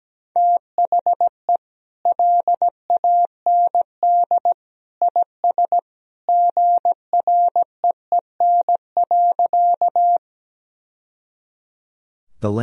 Morse code, heard as a keyed tone, about 17 wpm, 700 Hz. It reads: THE LAND IS GREEN.
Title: Sentences from Top 1000 Words - Review Part 2 - 17wpm